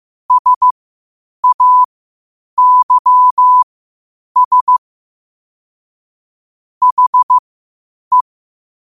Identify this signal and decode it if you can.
Morse code, heard as a keyed tone, about 15 wpm, spaced as for 8 wpm, 1000 Hz. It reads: SAYS HE
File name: qotdfiles / 8wpm